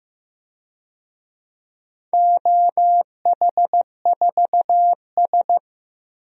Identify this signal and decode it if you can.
Morse code, heard as a keyed tone, about 15 wpm, 700 Hz. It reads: OH4S
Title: DXCC-100 Call Signs - 4 Characters - Repeated 3x - 15wpm